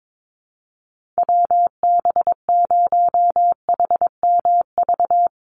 Morse code, heard as a keyed tone, about 22 wpm, 700 Hz. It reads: WB0HMV
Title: USA Call Signs 22wpm